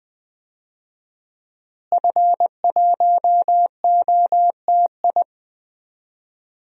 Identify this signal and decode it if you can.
Morse code, heard as a keyed tone, about 21 wpm, 700 Hz. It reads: F1OTI
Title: DXCC-100 Call Signs - 5 Characters - Repeated 2x - 20wpm